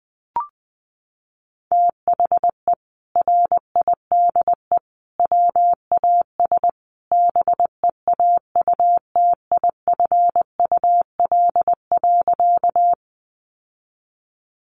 Morse code, heard as a keyed tone, about 20 wpm, 700 Hz. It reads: THE RIDE WAS BEAUTIFUL.
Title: Sentences from Top 1200 Words 20wpm